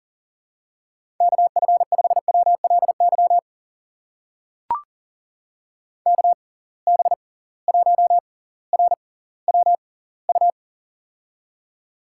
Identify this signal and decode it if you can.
Morse code, heard as a keyed tone, about 40 wpm, spaced as for 20 wpm, 700 Hz. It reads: KF5WLY KB1RWU
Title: USA Call Signs - 6 Characters - 6x Character Spacing - 40wpm